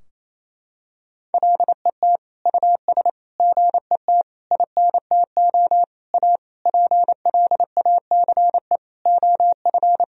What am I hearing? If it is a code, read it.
Morse code, heard as a keyed tone, about 28 wpm, 700 Hz. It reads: LET US GET INTO A PLACE OF
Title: Sentences from Top 1000 Words - Review Part 4 - 28wpm